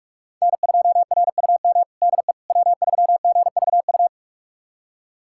Morse code, heard as a keyed tone, about 46 wpm, 700 Hz. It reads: N2RUM DE W3GVU
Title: USA Callsign DE Callsign - 5 Characters - 45wpm